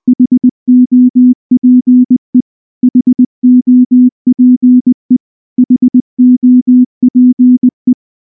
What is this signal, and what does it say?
Morse code, heard as a keyed tone, about 19 wpm, 250 Hz. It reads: HOPE HOPE HOPE
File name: morse / morse-training-20WPM-3